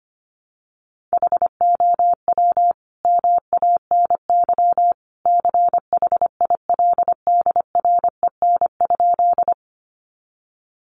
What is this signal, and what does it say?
Morse code, heard as a keyed tone, about 25 wpm, 700 Hz. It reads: HOW MANY CHILDREN?
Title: Sentences from Top 300 Words 25wpm